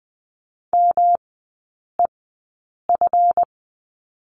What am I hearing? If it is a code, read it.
Morse code, heard as a keyed tone, about 20 wpm, spaced as for 8 wpm, 700 Hz. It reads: MEF
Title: 138 Lesson 23e - Letter f - Three Characters 20wpm